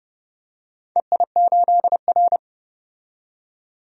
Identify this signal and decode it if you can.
Morse code, heard as a keyed tone, about 32 wpm, 700 Hz. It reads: EI8R
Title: DXCC-100 Call Signs - 4 Characters - 30wpm